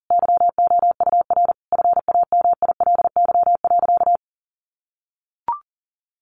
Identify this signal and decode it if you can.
Morse code, heard as a keyed tone, about 40 wpm, 700 Hz. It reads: YOUR FAMILY.
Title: Sentences from Top 1400 Words 40wpm